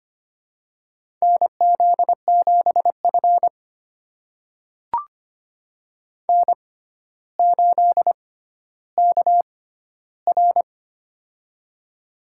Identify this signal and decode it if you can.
Morse code, heard as a keyed tone, about 25 wpm, spaced as for 13 wpm, 700 Hz. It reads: NZ7F N8KR
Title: USA Call Signs - 4 Characters - 6x Character Spacing - 25wpm